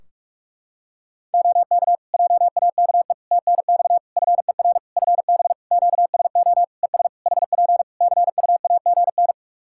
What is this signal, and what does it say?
Morse code, heard as a keyed tone, about 46 wpm, 700 Hz. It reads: OK JAKE TNX FER FB QSO ES HP CUAGN
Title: Practice QSO - Exchange 4 - Ending - 45wpm